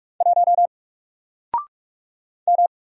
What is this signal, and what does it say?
Morse code, heard as a keyed tone, about 44 wpm, 700 Hz. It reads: 1 M
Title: Single Letter-Number Alt 45wpm